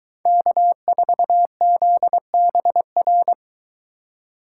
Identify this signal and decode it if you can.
Morse code, heard as a keyed tone, about 22 wpm, 700 Hz. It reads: K4ZBR